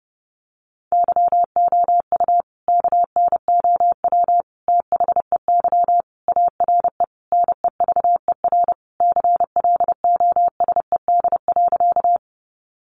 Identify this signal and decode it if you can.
Morse code, heard as a keyed tone, about 30 wpm, 700 Hz. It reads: YOU KNOW THEY ARE NEVER CLOSED.
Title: Sentences from Top 1400 Words 30wpm